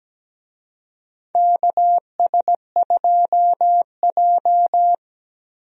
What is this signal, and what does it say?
Morse code, heard as a keyed tone, about 17 wpm, 700 Hz. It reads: KS2J
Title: USA Call Signs - 4 Characters - 17wpm